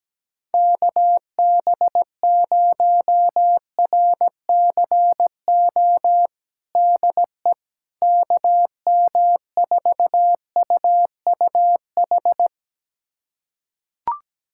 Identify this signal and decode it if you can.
Morse code, heard as a keyed tone, about 17 wpm, 700 Hz. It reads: KB0RCO DE KM4UUH
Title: USA Callsign DE Callsign - 6 Characters - 17wpm